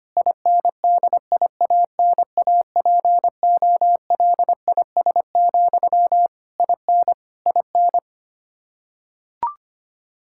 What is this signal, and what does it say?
Morse code, heard as a keyed tone, about 25 wpm, 700 Hz. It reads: INDIANAPOLIS, IN IN
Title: Practice QSO - Exchange 1 - Beginning - 25wpm